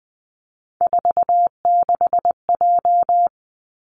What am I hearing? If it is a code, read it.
Morse code, heard as a keyed tone, about 20 wpm, 700 Hz. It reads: 46J
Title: Three Challenging Letter-Number Combinations 20wpm